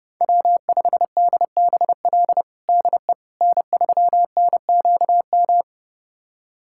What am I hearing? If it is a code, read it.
Morse code, heard as a keyed tone, about 32 wpm, 700 Hz. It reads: W5DBL DE N3NQM
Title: USA Callsign DE Callsign - 5 Characters - 30wpm